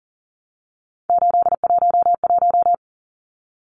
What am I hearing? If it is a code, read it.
Morse code, heard as a keyed tone, about 40 wpm, 700 Hz. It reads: 811
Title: Numbers - 3-Digit 40wpm